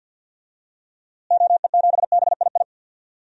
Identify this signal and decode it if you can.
Morse code, heard as a keyed tone, about 53 wpm, 700 Hz. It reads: OE7BII
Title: DXCC-100 Call Signs - 6 Characters - Repeated 3x - 50wpm